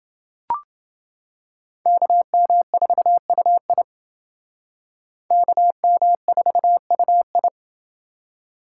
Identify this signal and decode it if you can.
Morse code, heard as a keyed tone, about 29 wpm, 700 Hz. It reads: KM4UI KM4UI